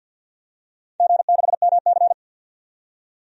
Morse code, heard as a keyed tone, about 53 wpm, 700 Hz. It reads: G6MC
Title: DXCC-100 Call Signs - 4 Characters - 50wpm